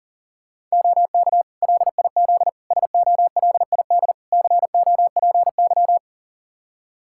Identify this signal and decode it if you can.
Morse code, heard as a keyed tone, about 40 wpm, 700 Hz. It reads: OK LIZ SOLID COPY
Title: K1USN SST QSO Practice 40wpm